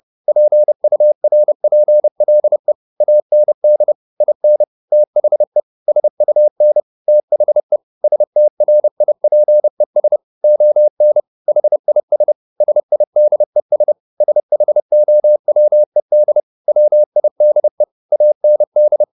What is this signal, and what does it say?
Morse code, heard as a keyed tone, about 30 wpm, 600 Hz. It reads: PURPLE AND IN THE SUN THE STRIPES ON HIS SIDES SHOWED WIDE AND
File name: the old man and the sea / the old man and the sea 1 30wpm